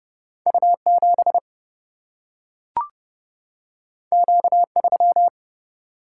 Morse code, heard as a keyed tone, about 30 wpm, 700 Hz. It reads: U7 Q3
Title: Single-Letter Single-Number 30wpm